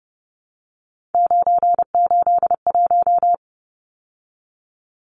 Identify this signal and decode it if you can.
Morse code, heard as a keyed tone, about 29 wpm, 700 Hz. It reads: 981
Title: Numbers - 3-Digit 30wpm